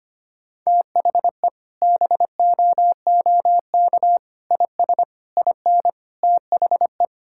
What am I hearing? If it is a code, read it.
Morse code, heard as a keyed tone, about 25 wpm, 700 Hz. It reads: THE BOOK IS IN THE
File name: practice / 25wpm / Sentences with 3-Word Phrases - 25wpm